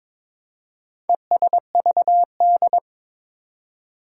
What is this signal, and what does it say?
Morse code, heard as a keyed tone, about 21 wpm, 700 Hz. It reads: ESVD
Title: Four Challenging Letter-Number Combinations 22wpm